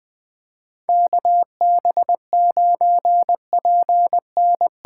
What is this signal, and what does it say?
Morse code, heard as a keyed tone, about 20 wpm, 700 Hz. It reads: KB9PN
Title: USA Call Signs - 8x Character Spacing - 20wpm